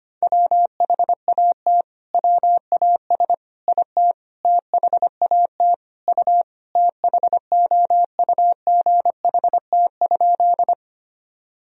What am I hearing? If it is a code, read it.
Morse code, heard as a keyed tone, about 25 wpm, 700 Hz. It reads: WHAT WAS IT THAT U THOUGHT?
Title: Sentences from Top 300 Words and Abbreviations 25wpm